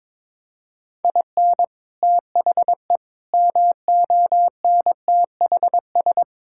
Morse code, heard as a keyed tone, about 22 wpm, 700 Hz. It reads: IN THE MONTHS
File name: practice / 22wpm / Sets of 7 Words - 22wpm